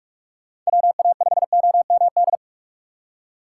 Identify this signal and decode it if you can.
Morse code, heard as a keyed tone, about 42 wpm, 700 Hz. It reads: WA5OMD
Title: USA Call Signs - 6x Character Spacing - 45wpm